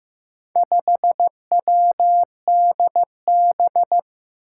Morse code, heard as a keyed tone, about 15 wpm, 700 Hz. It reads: 5WDB
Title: Four Challenging Letter-Number Combinations 15wpm